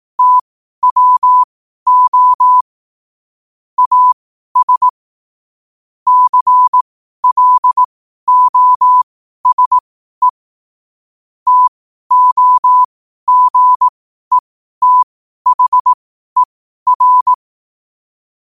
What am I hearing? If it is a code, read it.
Morse code, heard as a keyed tone, about 18 wpm, spaced as for 12 wpm, 1000 Hz. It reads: TWO AS CLOSE TOGETHER
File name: qotdfiles / 12wpm